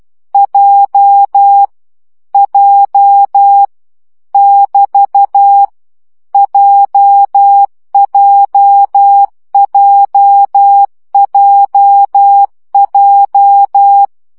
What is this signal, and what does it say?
Morse code, heard as a keyed tone, about 12 wpm, 800 Hz. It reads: J J <BT> JJJJJ
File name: CW gyakorlas j 10 baud